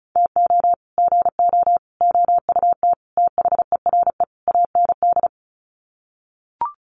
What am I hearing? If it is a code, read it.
Morse code, heard as a keyed tone, about 34 wpm, 700 Hz. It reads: TO GO OUT THERE AND
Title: Sets of 5 Words Encore2 35wpm